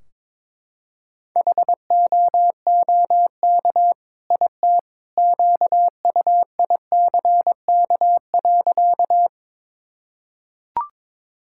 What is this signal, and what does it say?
Morse code, heard as a keyed tone, about 22 wpm, 700 Hz. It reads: HOOK IT QUICK.
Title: Sentences from Top 1600 Words 22wpm